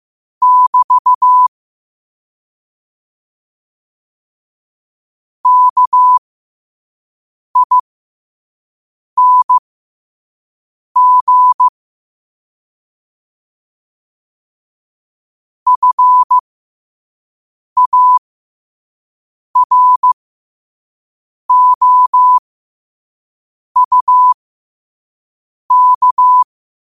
Morse code, heard as a keyed tone, about 15 wpm, spaced as for 5 wpm, 1000 Hz. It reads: <BT> KING FAROUK
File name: qotdfiles / 5wpm